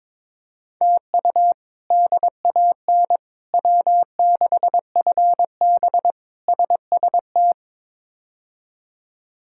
Morse code, heard as a keyed tone, about 22 wpm, 700 Hz. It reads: TU DAN W6FB SST
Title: K1USN SST QSO Practice - 22wpm